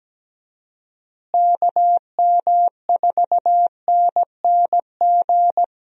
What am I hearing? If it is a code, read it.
Morse code, heard as a keyed tone, about 17 wpm, 700 Hz. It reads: KM4NNG
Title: USA Call Signs - 6 Characters - Repeated 2x - 17wpm